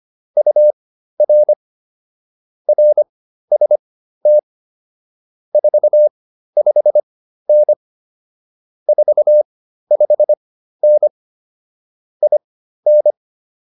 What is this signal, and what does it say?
Morse code, heard as a keyed tone, about 25 wpm, spaced as for 13 wpm, 600 Hz. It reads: UR RST 45N 45N IN
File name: QSO104 13